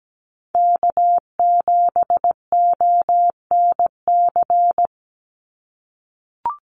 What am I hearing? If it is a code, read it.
Morse code, heard as a keyed tone, about 17 wpm, 700 Hz. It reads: K7ONC
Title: USA Call Signs 17wpm